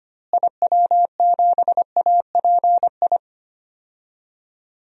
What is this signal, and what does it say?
Morse code, heard as a keyed tone, about 25 wpm, 700 Hz. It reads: IW7API
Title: DXCC-100 Call Signs - 6 Characters - Repeated 3x - 25wpm